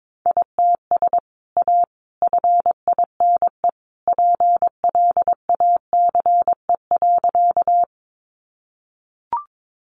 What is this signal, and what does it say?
Morse code, heard as a keyed tone, about 22 wpm, 700 Hz. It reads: ITS A FINE PLACE.
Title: Sentences from Top 600 Words 22wpm